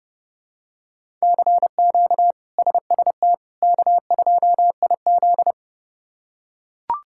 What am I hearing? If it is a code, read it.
Morse code, heard as a keyed tone, about 32 wpm, 700 Hz. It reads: CQ SST K2IZ
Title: K1USN SST QSO Practice - 30wpm